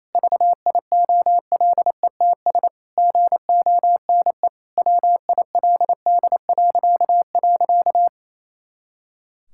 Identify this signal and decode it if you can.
Morse code, heard as a keyed tone, about 28 wpm, 700 Hz. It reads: VIOLETS GONE WILD..
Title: Pangram Challenge v2 - 28wpm